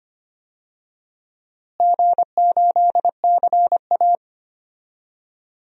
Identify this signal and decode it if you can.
Morse code, heard as a keyed tone, about 25 wpm, 700 Hz. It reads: G8CA
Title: DXCC-100 Call Signs - 4 Characters - Repeated 2x - 25wpm